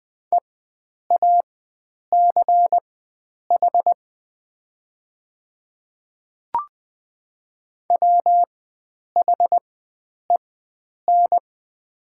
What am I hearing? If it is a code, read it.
Morse code, heard as a keyed tone, about 21 wpm, spaced as for 9 wpm, 700 Hz. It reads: EACH WHEN
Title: 4x Character Spacing - Top 100 Words - 20wpm